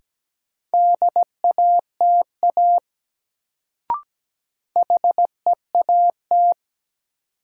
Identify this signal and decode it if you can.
Morse code, heard as a keyed tone, about 17 wpm, 700 Hz. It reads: DATA HEAT
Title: Sentences from Top 1000 Words - Review Part 3 - New Word Drill - 17wpm